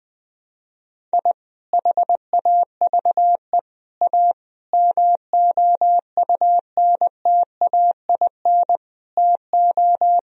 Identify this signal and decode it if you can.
Morse code, heard as a keyed tone, about 20 wpm, 700 Hz. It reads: I HAVE A MOUNTAIN TO
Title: Sentences from Top 1000 Words - Review Part 1 - 20wpm